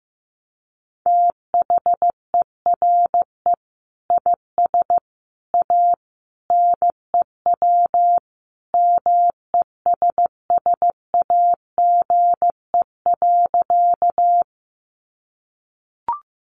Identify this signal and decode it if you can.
Morse code, heard as a keyed tone, about 15 wpm, 700 Hz. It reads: THERE IS A NEW MESSAGE.
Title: Sentences from Top 1100 Words 15wpm